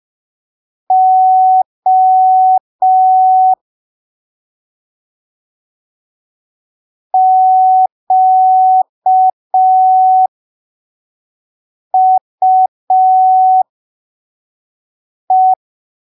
Morse code, heard as a keyed tone, about 5 wpm, spaced as for 3 wpm, 750 Hz. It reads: O QUE